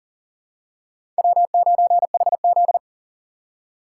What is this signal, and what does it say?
Morse code, heard as a keyed tone, about 40 wpm, 700 Hz. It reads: W9HZ
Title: USA Call Signs - 4 Characters - 4x Character Spacing - 40wpm